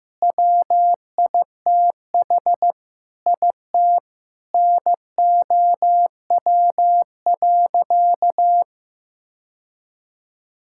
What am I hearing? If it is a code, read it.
Morse code, heard as a keyed tone, about 15 wpm, 700 Hz. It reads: WITH IT NOW.
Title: Sentences from Top 1000 Words - Review Part 6 - 15wpm